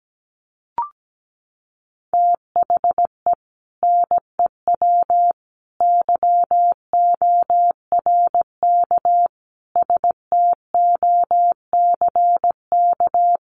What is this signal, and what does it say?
Morse code, heard as a keyed tone, about 17 wpm, 700 Hz. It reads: THE NEW YORK STOCK
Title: Sets of 5 Words 17wpm